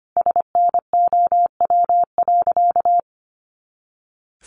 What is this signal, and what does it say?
Morse code, heard as a keyed tone, about 25 wpm, 700 Hz. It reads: SNOW.
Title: Easy Phrases 25wpm